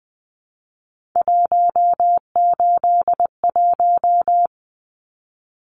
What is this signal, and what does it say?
Morse code, heard as a keyed tone, about 20 wpm, 700 Hz. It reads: 181
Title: Numbers - 3-Digit 20wpm